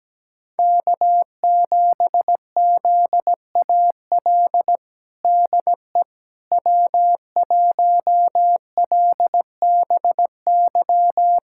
Morse code, heard as a keyed tone, about 17 wpm, 700 Hz. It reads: K7ZAL DE W1LBY
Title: USA Callsign DE Callsign - 5 Characters - 17wpm